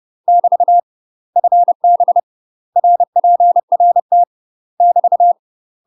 Morse code, heard as a keyed tone, about 30 wpm, 700 Hz. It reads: <BT> FB RPRT <BT>
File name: UK MASTER QSOs TRAINING / 30WPM / UK MASTER QSOs 0016